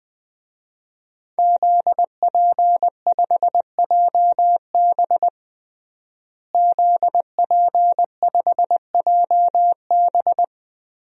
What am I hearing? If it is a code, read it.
Morse code, heard as a keyed tone, about 20 wpm, 700 Hz. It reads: ZP5JB ZP5JB
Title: DXCC-100 Call Signs - 5 Characters - Repeated 2x - 20wpm